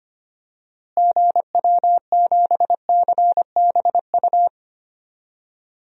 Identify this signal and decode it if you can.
Morse code, heard as a keyed tone, about 25 wpm, 700 Hz. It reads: GW7CBU
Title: DXCC-100 Call Signs - 6 Characters - Repeated 2x - 25wpm